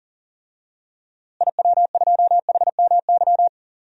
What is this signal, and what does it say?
Morse code, heard as a keyed tone, about 40 wpm, 700 Hz. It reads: IW2HMY
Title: DXCC-100 Call Signs - 6 Characters - Repeated 2x - 40wpm